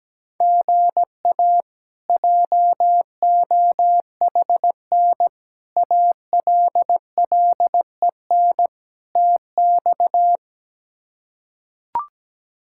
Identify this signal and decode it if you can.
Morse code, heard as a keyed tone, about 17 wpm, 700 Hz. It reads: GA JOHN ALLEN TX
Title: K1USN SST QSO Practice - 17wpm